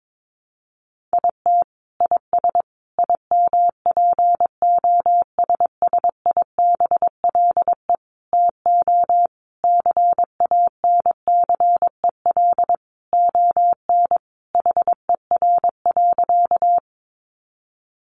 Morse code, heard as a keyed tone, about 22 wpm, 700 Hz. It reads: IT IS IMPOSSIBLE TO CANCEL ON HER.
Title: Sentences from Top 1200 Words 22wpm